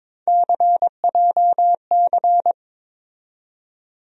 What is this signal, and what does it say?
Morse code, heard as a keyed tone, about 22 wpm, 700 Hz. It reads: CJC
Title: USA Call Sign Postfixes - 3 Letters 22wpm